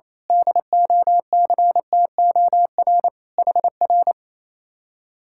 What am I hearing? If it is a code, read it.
Morse code, heard as a keyed tone, about 26 wpm, 700 Hz. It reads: DOCTOR HR
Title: Speed-Racing Common QSO Phrases - 28wpm